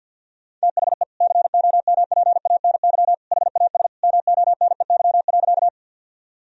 Morse code, heard as a keyed tone, about 50 wpm, 700 Hz. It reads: THE COMPANY HAS MONEY.
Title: Sentences from Top 1000 Words - Review Part 3 - 50wpm